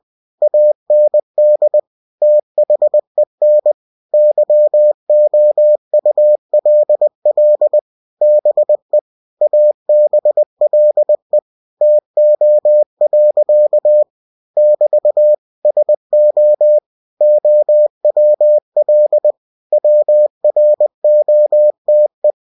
Morse code, heard as a keyed tone, about 20 wpm, 600 Hz. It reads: AND THEN YOULL BE ABLE TO. <BT> SO OWL WROTE
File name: cw-books / winnie-the-pooh / winnie-20wpm-page-0052